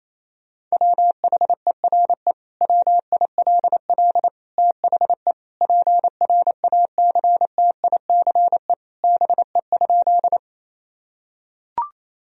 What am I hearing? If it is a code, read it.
Morse code, heard as a keyed tone, about 28 wpm, 700 Hz. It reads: WHERE WILL THE PRACTICE BE?
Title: Sentences from Top 1000 Words - Review Part 4 - 28wpm